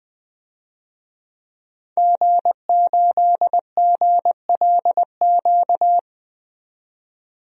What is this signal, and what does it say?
Morse code, heard as a keyed tone, about 20 wpm, 700 Hz. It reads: G8GLQ
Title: DXCC-100 Call Signs - 5 Characters - 20wpm